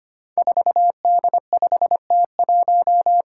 Morse code, heard as a keyed tone, about 25 wpm, 700 Hz. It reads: 4D5T1
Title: Five Challenging Letter-Number Combinations 25wpm